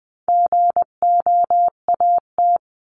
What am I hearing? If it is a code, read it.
Morse code, heard as a keyed tone, about 20 wpm, 700 Hz. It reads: GOAT
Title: 259 Lesson 42d - Words 18-20wpm